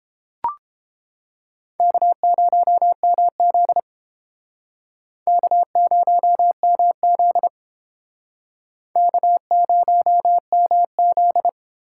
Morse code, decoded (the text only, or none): K0MZ K0MZ K0MZ